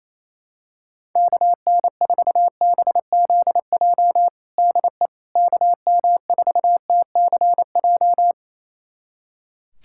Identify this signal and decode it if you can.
Morse code, heard as a keyed tone, about 28 wpm, 700 Hz. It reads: KN4BZJ DE KM4TCJ
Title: USA Callsign DE Callsign - 6 Characters - 28wpm